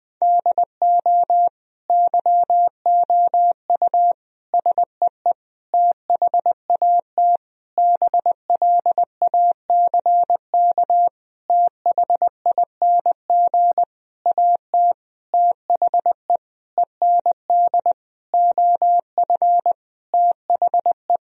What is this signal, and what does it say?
Morse code, heard as a keyed tone, about 20 wpm, 700 Hz. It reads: DO YOU SEE THAT BLACK THING AT THE END OF THE
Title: Sentences from Top 1000 Words - Review Part 2 - 20wpm